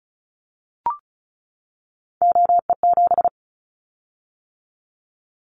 Morse code, heard as a keyed tone, about 36 wpm, 700 Hz. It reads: OE7
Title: Callsign Prefixes for Top 100 DXCC - 35wpm